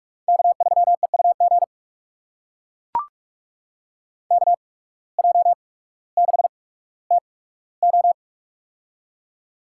Morse code, heard as a keyed tone, about 42 wpm, spaced as for 19 wpm, 700 Hz. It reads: K3EUG KJ6TO
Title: USA Call Signs - 5 Characters - 8x Character Spacing - 45wpm